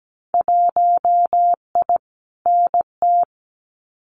Morse code, heard as a keyed tone, about 17 wpm, 700 Hz. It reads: 1I NT
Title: Winter Field Day Exchanges Encore 17wpm